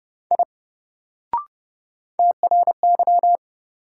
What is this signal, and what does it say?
Morse code, heard as a keyed tone, about 30 wpm, 700 Hz. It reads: I TRY
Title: Sentences from Top 200 Words - New Word Drill - 30wpm